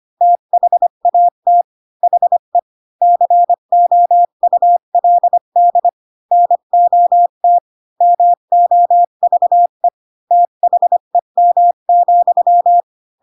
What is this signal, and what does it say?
Morse code, heard as a keyed tone, about 25 wpm, 700 Hz. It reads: THAT HE COULD NOT MOVE THEM,
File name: Pinocchio / Pinocchio ch3 20 25 ik1hge